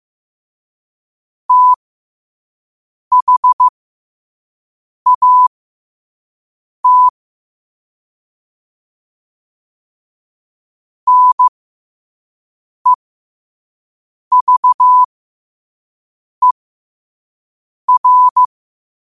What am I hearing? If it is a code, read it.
Morse code, heard as a keyed tone, about 15 wpm, spaced as for 5 wpm, 1000 Hz. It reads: THAT NEVER
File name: qotdfiles / 5wpm